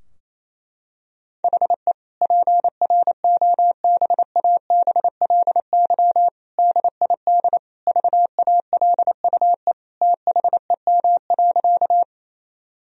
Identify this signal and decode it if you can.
Morse code, heard as a keyed tone, about 27 wpm, 700 Hz. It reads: HE PROBABLY DID VALUE THEM.
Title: Sentences from Top 1000 Words - Review Part 4 - 28wpm